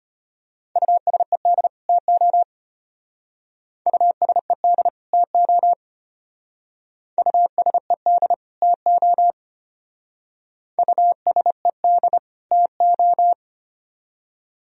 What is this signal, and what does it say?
Morse code, heard as a keyed tone, about 32 wpm, 700 Hz. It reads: USED TO USED TO USED TO USED TO